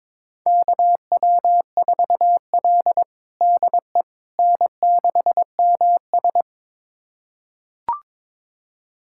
Morse code, heard as a keyed tone, about 22 wpm, 700 Hz. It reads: KW4L DE N6MS
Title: USA Callsign DE Callsign - 4 Characters - 22wpm